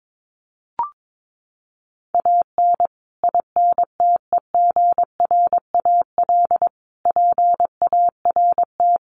Morse code, heard as a keyed tone, about 22 wpm, 700 Hz. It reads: AN INTEGRAL PART
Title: Sets of 5 Words 22wpm